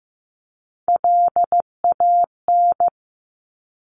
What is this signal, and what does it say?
Morse code, heard as a keyed tone, about 15 wpm, 700 Hz. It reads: LAN
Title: Top Three Letter Combinations in Words 15wpm